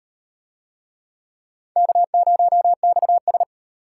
Morse code, heard as a keyed tone, about 38 wpm, 700 Hz. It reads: K0XS